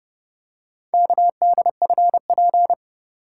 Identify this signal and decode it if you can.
Morse code, heard as a keyed tone, about 29 wpm, 700 Hz. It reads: KDFP